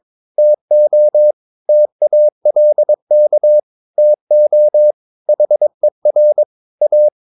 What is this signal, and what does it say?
Morse code, heard as a keyed tone, about 22 wpm, 600 Hz. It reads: TO TALK TO HER A
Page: Romanen Never af Ken Follet på engelsk.